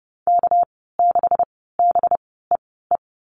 Morse code, heard as a keyed tone, about 30 wpm, spaced as for 17 wpm, 700 Hz. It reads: K6BEE
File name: Lesson 37g - Letter x - Call Signs 30wpm